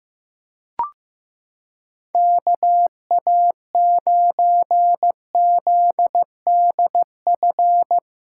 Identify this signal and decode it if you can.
Morse code, heard as a keyed tone, about 15 wpm, 700 Hz. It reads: KA9ZDF
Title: USA Callsign DE Callsign - 6 Characters - 15wpm